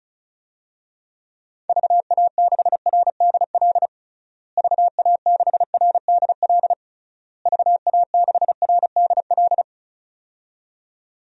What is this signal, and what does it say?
Morse code, heard as a keyed tone, about 36 wpm, 700 Hz. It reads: VA6RDL VA6RDL VA6RDL
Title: DXCC-100 Call Signs - 6 Characters - Repeated 3x - 35wpm